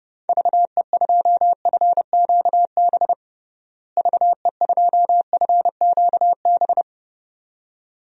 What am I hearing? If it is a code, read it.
Morse code, heard as a keyed tone, about 30 wpm, 700 Hz. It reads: VE2FQB VE2FQB
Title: DXCC-100 Call Signs - 6 Characters - Repeated 2x - 30wpm